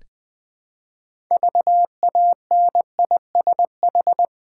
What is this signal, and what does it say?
Morse code, heard as a keyed tone, about 20 wpm, 700 Hz. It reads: VANISH
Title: Build A Word - 3 Letters At A Time - 6-Letter Words - 20wpm